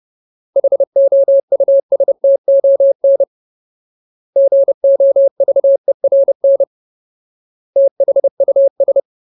Morse code, heard as a keyed tone, about 30 wpm, 550 Hz. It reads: HOUSTON GOVERN THUS
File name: Ultimate Word List / 30WPM Ultimate